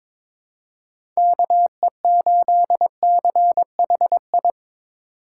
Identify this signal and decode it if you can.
Morse code, heard as a keyed tone, about 22 wpm, 700 Hz. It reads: KE8CHI
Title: USA Call Signs - 4x Character Spacing - 22wpm